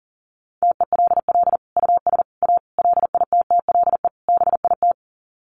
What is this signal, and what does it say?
Morse code, heard as a keyed tone, about 40 wpm, 700 Hz. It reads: TELL US A LITTLE BIT
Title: Sets of 5 Words Encore2 40wpm